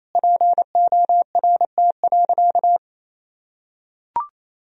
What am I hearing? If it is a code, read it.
Morse code, heard as a keyed tone, about 28 wpm, 700 Hz. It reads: PORT.
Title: Sentences from Top 1000 Words - Review Part 1 - 28wpm